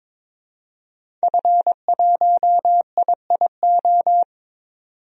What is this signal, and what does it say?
Morse code, heard as a keyed tone, about 22 wpm, 700 Hz. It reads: F1IIO
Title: DXCC-100 Call Signs - 5 Characters - 22wpm